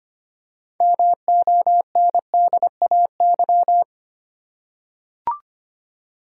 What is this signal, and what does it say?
Morse code, heard as a keyed tone, about 25 wpm, 700 Hz. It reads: MONDAY
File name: practice / 25wpm / Easy Words with Hints - 25wpm